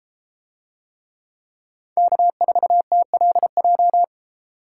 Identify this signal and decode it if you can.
Morse code, heard as a keyed tone, about 32 wpm, 700 Hz. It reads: K4TLJ